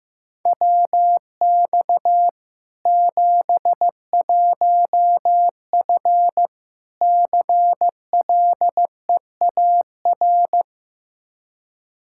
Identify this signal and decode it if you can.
Morse code, heard as a keyed tone, about 15 wpm, 700 Hz. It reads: WX 71F CLEAR
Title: K1USN SST QSO Practice 15wpm